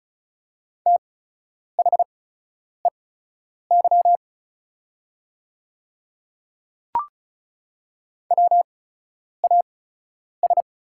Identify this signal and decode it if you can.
Morse code, heard as a keyed tone, about 36 wpm, spaced as for 10 wpm, 700 Hz. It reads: THEY WAS
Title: 8x Character Spacing - Top 100 Words - 35wpm